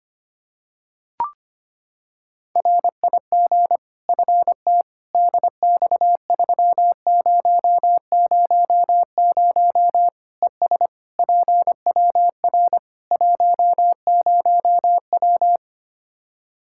Morse code, heard as a keyed tone, about 25 wpm, 700 Hz. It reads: RIG FT DX3000 ES PWR 10W
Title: Practice QSO - Exchange 2 - Info - 25wpm